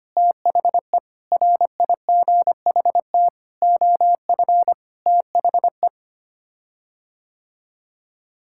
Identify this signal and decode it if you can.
Morse code, heard as a keyed tone, about 25 wpm, 700 Hz. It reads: THE RIGHT OF THE
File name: Sets of 9 Words - 25wpm